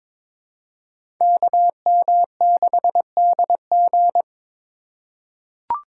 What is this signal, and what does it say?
Morse code, heard as a keyed tone, about 22 wpm, 700 Hz. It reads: KM6DG
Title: USA Call Signs - 5 Characters - 3x Character Spacing - 22wpm